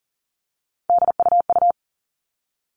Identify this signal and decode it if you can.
Morse code, heard as a keyed tone, about 40 wpm, 700 Hz. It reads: DUU
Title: All Cut Numbers - 3-Digit 40wpm